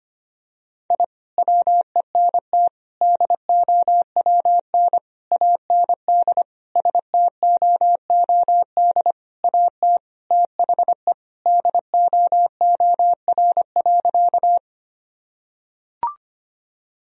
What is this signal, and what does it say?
Morse code, heard as a keyed tone, about 25 wpm, 700 Hz. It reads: I WENT DOWN AND STOOD AT THE DOOR.
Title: Sentences from Top 1000 Words - Review Part 2 - 25wpm